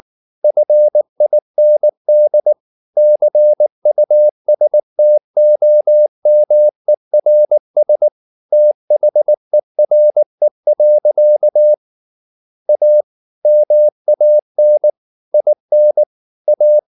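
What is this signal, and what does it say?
Morse code, heard as a keyed tone, about 19 wpm, 600 Hz. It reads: FIND CUSTOMERS THERE. A MAN IN A
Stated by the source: Morse practice at 18 wpm, English text, from CW med Gnister.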